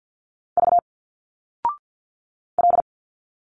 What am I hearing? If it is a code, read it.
Morse code, heard as a keyed tone, about 51 wpm, 700 Hz. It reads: V L
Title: Single Letters 50wpm